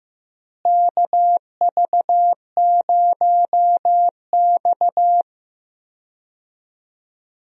Morse code, heard as a keyed tone, about 15 wpm, 700 Hz. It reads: KV0X
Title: USA Call Signs - 4 Characters - Repeated 2x - 15wpm